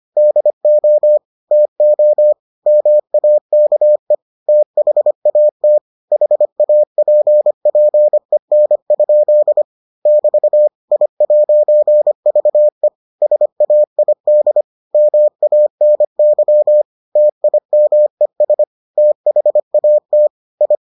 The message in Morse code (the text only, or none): DO TO MAKE THAT HAPPEN? <BT> I'VE SAID MANY TIMES THAT I